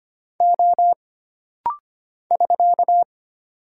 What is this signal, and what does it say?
Morse code, heard as a keyed tone, about 25 wpm, 700 Hz. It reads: O <SK>
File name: Single Character (Letter:Number:Punctuation:Prosign) - Rapid-Fire - 25wpm